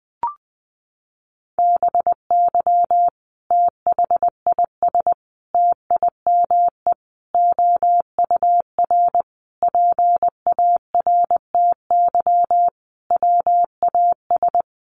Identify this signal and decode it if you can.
Morse code, heard as a keyed tone, about 20 wpm, 700 Hz. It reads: BY THIS TIME OUR PARTY WAS
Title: Sentences from Top 1400 Words 20wpm